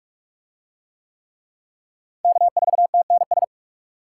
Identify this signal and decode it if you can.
Morse code, heard as a keyed tone, about 40 wpm, 700 Hz. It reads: K4TNS